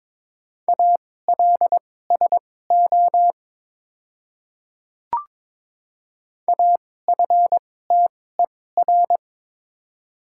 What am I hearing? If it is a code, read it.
Morse code, heard as a keyed tone, about 22 wpm, spaced as for 16 wpm, 700 Hz. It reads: ALSO AFTER
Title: 2x Character Spacing - Top 300 Words - 22wpm